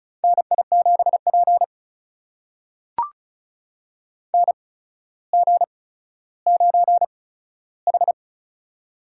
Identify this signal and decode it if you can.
Morse code, heard as a keyed tone, about 36 wpm, spaced as for 15 wpm, 700 Hz. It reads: NI7P NG9H
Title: USA Call Signs - 4 Characters - 8x Character Spacing - 35wpm